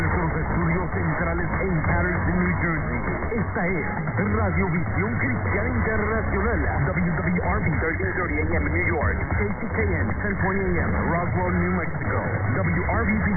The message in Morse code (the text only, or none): none